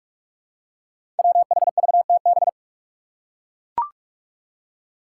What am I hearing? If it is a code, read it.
Morse code, heard as a keyed tone, about 44 wpm, 700 Hz. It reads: WHVTB
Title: Five Challenging Letter-Number Combinations 45wpm